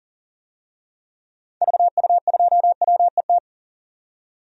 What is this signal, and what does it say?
Morse code, heard as a keyed tone, about 40 wpm, 700 Hz. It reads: VU2WET
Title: DXCC-100 Call Signs - 6 Characters - 40wpm